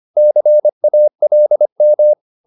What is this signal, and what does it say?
Morse code, heard as a keyed tone, about 25 wpm, 600 Hz. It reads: CALM
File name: the old man and the sea part 1 25wpm